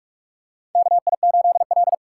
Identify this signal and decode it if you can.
Morse code, heard as a keyed tone, about 46 wpm, 700 Hz. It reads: KI8L
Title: USA Call Signs - 4 Characters - 4x Character Spacing - 45wpm